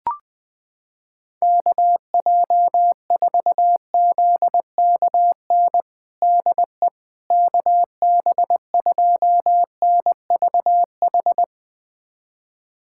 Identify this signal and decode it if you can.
Morse code, heard as a keyed tone, about 20 wpm, 700 Hz. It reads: KJ4ZKN DE KB2NVH
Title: USA Callsign DE Callsign - 6 Characters - 20wpm